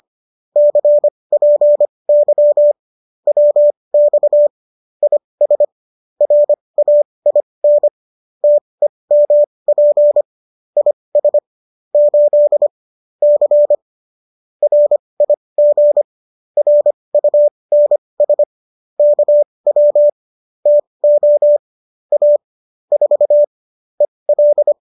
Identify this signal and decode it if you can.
Morse code, heard as a keyed tone, about 25 wpm, spaced as for 20 wpm, 590 Hz. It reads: CPY WX IS RAIN TEMP IS 8 C RIG RUNS KW TO A 4 EL